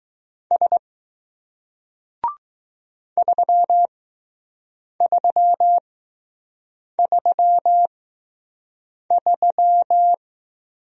Morse code, heard as a keyed tone, about 20 wpm, spaced as for 6 wpm, 700 Hz. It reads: S 3333